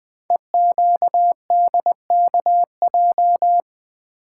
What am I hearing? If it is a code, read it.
Morse code, heard as a keyed tone, about 20 wpm, 700 Hz. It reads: EQDKJ
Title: Code Groups - 5 Letters - 20wpm